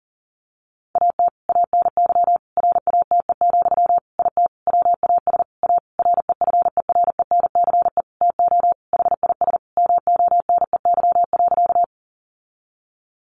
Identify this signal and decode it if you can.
Morse code, heard as a keyed tone, about 40 wpm, 700 Hz. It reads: AT ANY RATE, IT WAS A REFERENCE TO HIS MONEY.
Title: Sentences from Top 1200 Words 40wpm